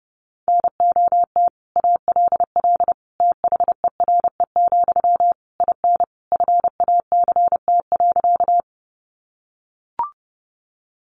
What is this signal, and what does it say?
Morse code, heard as a keyed tone, about 30 wpm, 700 Hz. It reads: NOT ALL THERE, IN FACT.
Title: Sentences from Top 600 Words 30wpm